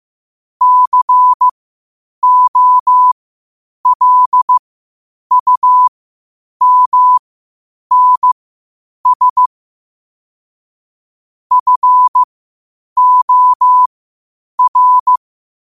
Morse code, heard as a keyed tone, about 15 wpm, spaced as for 8 wpm, 1000 Hz. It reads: COLUMNS FOR
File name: qotdfiles / 8wpm